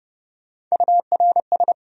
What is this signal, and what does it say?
Morse code, heard as a keyed tone, about 42 wpm, spaced as for 36 wpm, 700 Hz. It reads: URS
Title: Speed-Racing Abbreviations - 30wpm